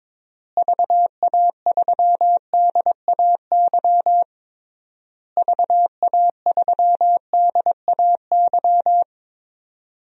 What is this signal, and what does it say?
Morse code, heard as a keyed tone, about 22 wpm, 700 Hz. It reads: VA3DAY VA3DAY
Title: DXCC-100 Call Signs - 6 Characters - Repeated 3x - 22wpm